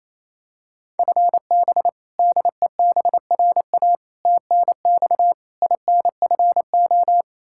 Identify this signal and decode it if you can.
Morse code, heard as a keyed tone, about 28 wpm, 700 Hz. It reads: FB DEBRA TNX INFO
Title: Practice QSO - Exchange 3 - More Info - 28wpm